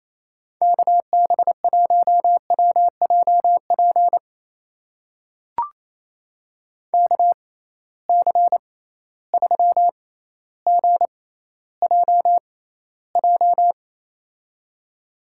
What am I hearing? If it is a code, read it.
Morse code, heard as a keyed tone, about 28 wpm, spaced as for 14 wpm, 700 Hz. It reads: KB1WJP KC3GJJ
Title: USA Call Signs - 6 Characters - 6x Character Spacing - 28wpm